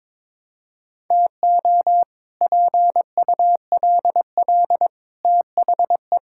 Morse code, heard as a keyed tone, about 22 wpm, 700 Hz. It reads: TO PULL THE
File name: practice / 22wpm / Sets of 8 Words - 22wpm